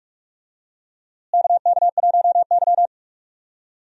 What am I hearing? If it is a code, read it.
Morse code, heard as a keyed tone, about 46 wpm, 700 Hz. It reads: KK1Y
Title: USA Call Signs - 4 Characters - Repeated 2x - 45wpm